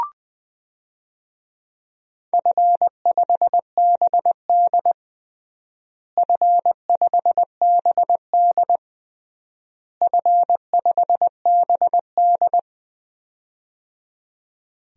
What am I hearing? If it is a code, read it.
Morse code, heard as a keyed tone, about 20 wpm, 700 Hz. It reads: F5BD F5BD F5BD
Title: DXCC-100 Call Signs - 4 Characters - Repeated 3x - 20wpm